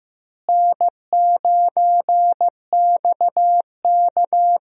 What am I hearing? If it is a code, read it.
Morse code, heard as a keyed tone, about 16 wpm, 700 Hz. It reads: N9XK
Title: USA Call Signs - 4 Characters - Repeated 2x - 15wpm